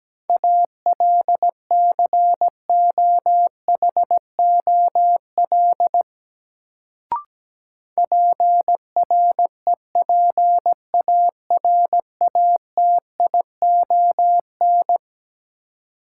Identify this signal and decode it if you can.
Morse code, heard as a keyed tone, about 17 wpm, 700 Hz. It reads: ALCOHOL PREPARATION
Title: Sentences from Top 1400 Words - New Word Drill - 17wpm